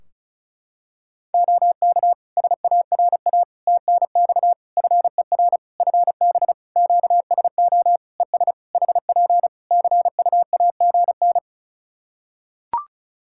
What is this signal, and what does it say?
Morse code, heard as a keyed tone, about 34 wpm, 700 Hz. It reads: OK SARA TNX FER FB QSO ES HP CUAGN
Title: Practice QSO - Exchange 4 - Ending - 35wpm